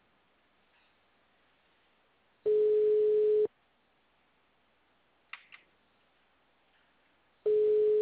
Ringback tone (Europe, 425 Hz).
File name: out- -2015- - - .71